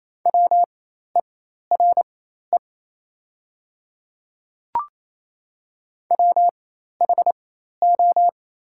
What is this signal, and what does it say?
Morse code, heard as a keyed tone, about 27 wpm, spaced as for 13 wpm, 690 Hz. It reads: WERE WHO
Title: 4x Character Spacing - Top 100 Words - 28wpm